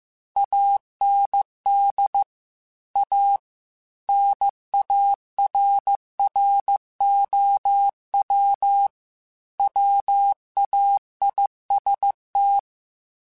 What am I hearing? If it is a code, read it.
Morse code, heard as a keyed tone, about 15 wpm, 800 Hz. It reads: AND A NARROW WAIST
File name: Chuckles / Chuckles-mp3s / MiddleAge @13